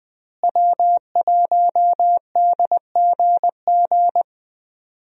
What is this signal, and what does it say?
Morse code, heard as a keyed tone, about 20 wpm, 700 Hz. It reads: W1DGG
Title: USA Call Signs - 5 Characters - Repeated 3x - 20wpm